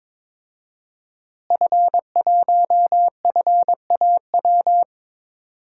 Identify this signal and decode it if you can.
Morse code, heard as a keyed tone, about 22 wpm, 700 Hz. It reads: F1FAW
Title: DXCC-100 Call Signs - 5 Characters - Repeated 3x - 22wpm